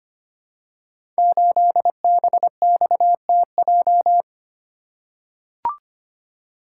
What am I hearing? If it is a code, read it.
Morse code, heard as a keyed tone, about 25 wpm, 700 Hz. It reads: 8BXTJ
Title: Code Groups - 5 Letter:Numbers - 25wpm